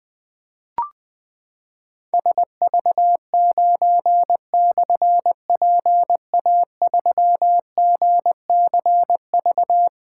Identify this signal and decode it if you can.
Morse code, heard as a keyed tone, about 20 wpm, 700 Hz. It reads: SV9/PA3GCV
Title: Difficult Call Signs 20wpm